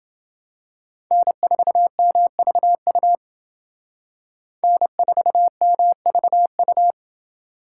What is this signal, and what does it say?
Morse code, heard as a keyed tone, about 28 wpm, 700 Hz. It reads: N4MVU N4MVU